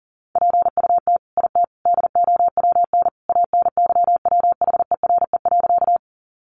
Morse code, heard as a keyed tone, about 40 wpm, 700 Hz. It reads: PUT IT DOWN ANYWHERE.
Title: Sentences from Top 1400 Words 40wpm